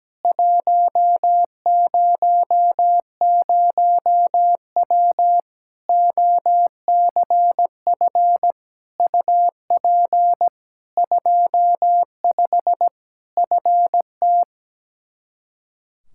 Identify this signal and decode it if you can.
Morse code, heard as a keyed tone, about 17 wpm, 700 Hz. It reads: 100W OCF UP 25 FT
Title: Practice QSO - Exchange 2 - Info - 17wpm